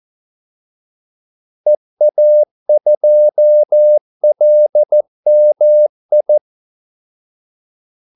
Morse code, heard as a keyed tone, about 14 wpm, 600 Hz. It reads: EA2LMI
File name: Callsigns-14wpm